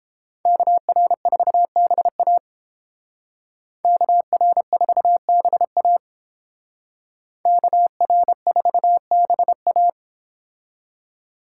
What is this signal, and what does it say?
Morse code, heard as a keyed tone, about 29 wpm, 700 Hz. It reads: KR4BA KR4BA KR4BA